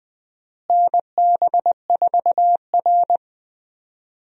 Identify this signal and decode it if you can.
Morse code, heard as a keyed tone, about 20 wpm, 700 Hz. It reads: NB4R
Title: USA Call Signs - 4 Characters - Repeated 2x - 20wpm